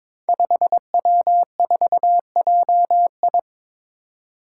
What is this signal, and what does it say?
Morse code, heard as a keyed tone, about 22 wpm, 700 Hz. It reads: 5W4JI
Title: Five Challenging Letter-Number Combinations 22wpm